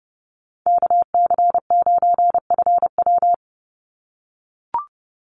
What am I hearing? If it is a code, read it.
Morse code, heard as a keyed tone, about 30 wpm, 700 Hz. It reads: KC9FW
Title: USA Call Signs 30wpm